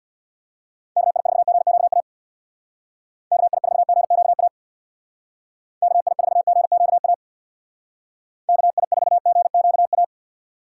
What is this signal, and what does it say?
Morse code, heard as a keyed tone, about 60 wpm, 700 Hz. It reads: KI4GQA KI4GQA KI4GQA KI4GQA
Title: Speed Racing - USA 2x3 Call Signs 50wpm